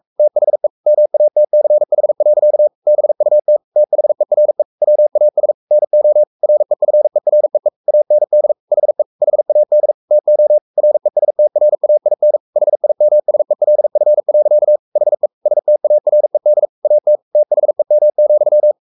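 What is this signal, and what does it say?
Morse code, heard as a keyed tone, about 44 wpm, 600 Hz. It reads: THE MATCH. BUT THERE WAS NO REFEREE AND HE HAD TO RESTRAIN HIMSELF. HE STARED AT THEM,
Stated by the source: Morse code at 43 wpm sending Ken Follett's novel Never in English, from CW med Gnister.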